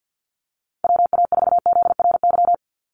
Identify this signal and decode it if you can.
Morse code, heard as a keyed tone, about 50 wpm, 700 Hz. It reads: WA4ZRY
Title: USA Call Signs 50wpm